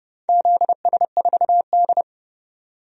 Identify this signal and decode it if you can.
Morse code, heard as a keyed tone, about 30 wpm, 700 Hz. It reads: ZS4D
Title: DXCC-100 Call Signs - 4 Characters - 30wpm